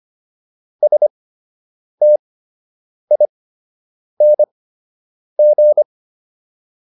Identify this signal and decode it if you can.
Morse code, heard as a keyed tone, about 25 wpm, spaced as for 8 wpm, 600 Hz. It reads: STING